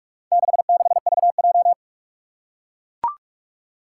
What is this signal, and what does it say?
Morse code, heard as a keyed tone, about 41 wpm, 700 Hz. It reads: 66VJ
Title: Four Challenging Letter-Number Combinations 45wpm